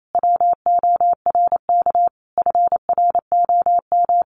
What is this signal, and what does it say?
Morse code, heard as a keyed tone, about 28 wpm, 700 Hz. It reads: WORK FROM
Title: Sentences from Top 1500 Words 28wpm